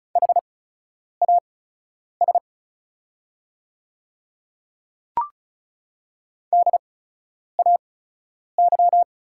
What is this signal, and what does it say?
Morse code, heard as a keyed tone, about 34 wpm, spaced as for 10 wpm, 700 Hz. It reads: HAS DAY
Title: 8x Character Spacing - Top 100 Words - 35wpm